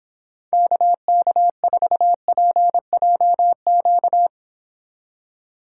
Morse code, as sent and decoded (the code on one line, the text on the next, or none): -.- -.- ....- .--. .--- --.-
KK4PJQ